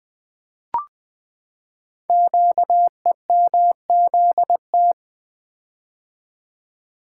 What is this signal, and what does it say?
Morse code, heard as a keyed tone, about 20 wpm, 700 Hz. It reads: QEMZT
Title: Code Groups - 5 Letters - 20wpm